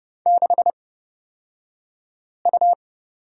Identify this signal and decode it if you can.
Morse code, heard as a keyed tone, about 30 wpm, spaced as for 5 wpm, 700 Hz. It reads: 6U
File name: Lesson 37d - Letter x - Two Characters 30wpm